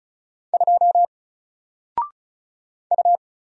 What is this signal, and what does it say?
Morse code, heard as a keyed tone, about 36 wpm, 700 Hz. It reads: 2 U
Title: Single Letter-Number Alt 35wpm